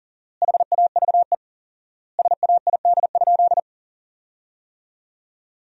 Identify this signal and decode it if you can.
Morse code, heard as a keyed tone, about 40 wpm, 700 Hz. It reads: HAVE SAID?
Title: 4x Word-Spacing - Sentences from Top 100 Words - 40wpm